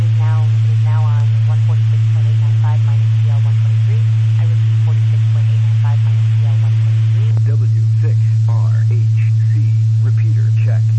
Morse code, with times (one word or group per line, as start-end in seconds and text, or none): none